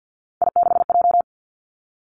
Morse code, heard as a keyed tone, about 50 wpm, 700 Hz. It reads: I6J
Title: Three Challenging Letter-Number Combinations 50wpm